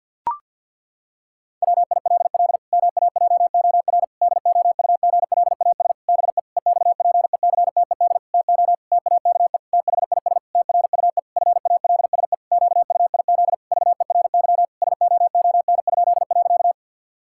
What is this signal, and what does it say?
Morse code, heard as a keyed tone, about 50 wpm, 700 Hz. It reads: WILL MAJOR DOUGLAS BE EXPECTED TO TAKE THIS TRUE FALSE QUIZ VERY SOON?.
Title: Pangram Challenge v2 - 50wpm